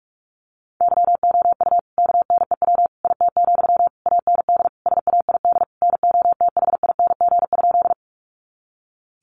Morse code, heard as a keyed tone, about 44 wpm, 700 Hz. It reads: YOU KNEW IT, AND SAID NOTHING?
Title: Sentences from Top 500 Words 45wpm